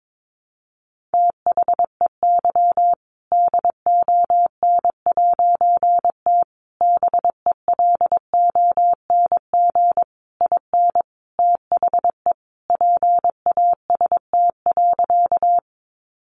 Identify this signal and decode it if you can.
Morse code, heard as a keyed tone, about 22 wpm, 700 Hz. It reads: THEY DON'T BELONG IN THE PAST.
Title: A Sound Of Thunder 22wpm